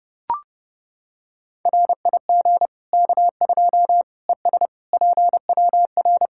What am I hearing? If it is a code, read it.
Morse code, heard as a keyed tone, about 32 wpm, 700 Hz. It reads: RIG K2 ES PWR
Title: Practice QSO - Exchange 2 - Info - 30wpm